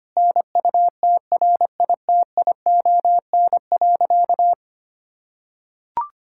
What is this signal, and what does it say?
Morse code, heard as a keyed tone, about 25 wpm, 700 Hz. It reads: NUTRITION.
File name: practice / 25wpm / Difficult Sentences with Hints - 25wpm